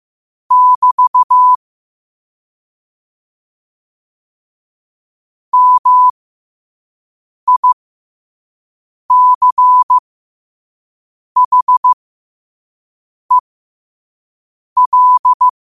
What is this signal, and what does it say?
Morse code, heard as a keyed tone, about 15 wpm, spaced as for 5 wpm, 1000 Hz. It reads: <BT> MICHEL
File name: qotdfiles / 5wpm